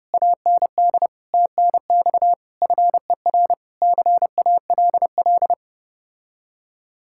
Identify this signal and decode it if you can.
Morse code, heard as a keyed tone, about 31 wpm, 700 Hz. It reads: AND TNX FER CALL